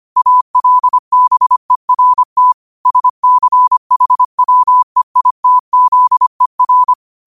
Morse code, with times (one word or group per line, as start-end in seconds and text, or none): 0.2-2.5 ALBERT
2.9-6.9 SCHWEITZER